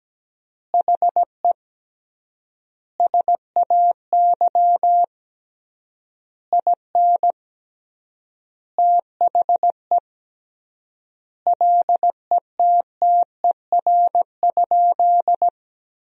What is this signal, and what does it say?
Morse code, heard as a keyed tone, about 17 wpm, 700 Hz. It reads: HE SAY IN THE LETTER?
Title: 3x Word-Spacing - Sentences from Top 300 Words - 17wpm